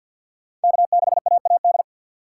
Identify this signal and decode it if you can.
Morse code, heard as a keyed tone, about 52 wpm, 700 Hz. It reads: K6AAD
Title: USA Call Signs - 6x Character Spacing - 50wpm